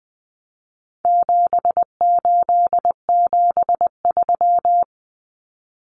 Morse code, heard as a keyed tone, about 20 wpm, 700 Hz. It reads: 7873
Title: Common Cut Numbers - 4-Digit 20wpm